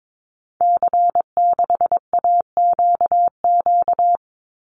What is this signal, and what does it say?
Morse code, heard as a keyed tone, about 22 wpm, 700 Hz. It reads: C6AQQ
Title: Call Signs Repeated 2x - 22wpm